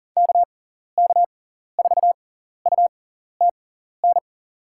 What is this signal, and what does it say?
Morse code, heard as a keyed tone, about 40 wpm, spaced as for 14 wpm, 700 Hz. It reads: KK4UTN
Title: USA Call Signs - 6 Characters - 6x Character Spacing - 40wpm